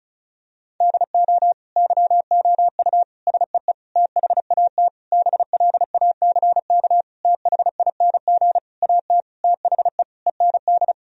Morse code, heard as a keyed tone, about 36 wpm, 700 Hz. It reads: DO YOU SEE THAT BLACK THING AT THE END
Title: Sentences from Top 1000 Words - Review Part 2 - 35wpm